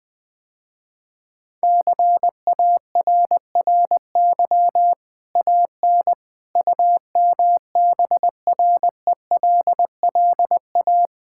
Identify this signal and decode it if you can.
Morse code, heard as a keyed tone, about 20 wpm, 700 Hz. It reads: CARRY AN UMBRELLA
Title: Easy Sentences with Hints - 20wpm